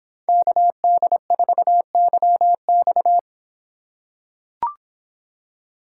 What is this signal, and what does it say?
Morse code, heard as a keyed tone, about 26 wpm, 700 Hz. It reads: KD4YX